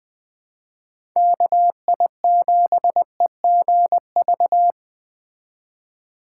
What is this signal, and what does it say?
Morse code, heard as a keyed tone, about 20 wpm, 700 Hz. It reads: KI7EGV
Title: Speed Racing - USA 2x3 Call Signs 17wpm